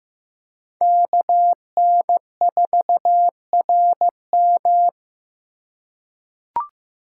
Morse code, heard as a keyed tone, about 15 wpm, 700 Hz. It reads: KN4RM
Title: USA Call Signs - 3x Character Spacing - 15wpm